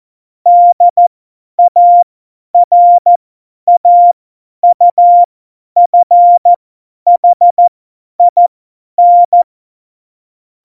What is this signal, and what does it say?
Morse code, heard as a keyed tone, about 14 wpm, spaced as for 10 wpm, 700 Hz. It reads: DARAUFHIN
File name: Morsekurs / Witze / Witz 27. Oktober 2020, 50 F70